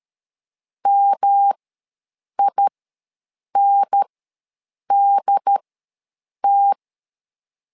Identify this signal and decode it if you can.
Morse code, heard as a keyed tone, about 13 wpm, spaced as for 7 wpm, 800 Hz. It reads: MINDT
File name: LEZIONI / 3 MWRO / 3 M ws 12wpm 1745